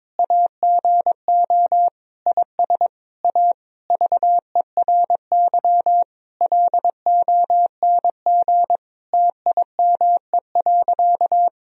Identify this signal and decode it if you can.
Morse code, heard as a keyed tone, about 22 wpm, 700 Hz. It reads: AGO IS A VERY LONG TIME.
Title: Sentences from Top 1000 Words - Review Part 3 - 22wpm